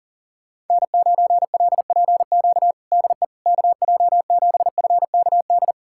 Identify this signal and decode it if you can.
Morse code, heard as a keyed tone, about 37 wpm, 700 Hz. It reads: N9LPQ DE KJ7FKD
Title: USA Callsign DE Callsign 40wpm